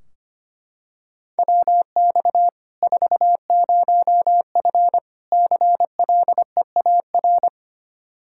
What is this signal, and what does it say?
Morse code, heard as a keyed tone, about 25 wpm, 700 Hz. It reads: WX 40F CLEAR
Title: Practice QSO - Exchange 2 - Info - 25wpm